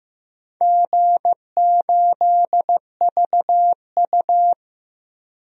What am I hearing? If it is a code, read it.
Morse code, heard as a keyed tone, about 15 wpm, 700 Hz. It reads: G8VU
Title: DXCC-100 Call Signs - 4 Characters - Repeated 3x - 15wpm